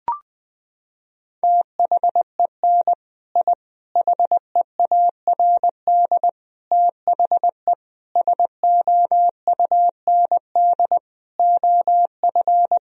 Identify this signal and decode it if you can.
Morse code, heard as a keyed tone, about 20 wpm, 700 Hz. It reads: THEN I HEARD THE SOUND OF
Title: Sentences from Top 1000 Words - Review Part 1 - 20wpm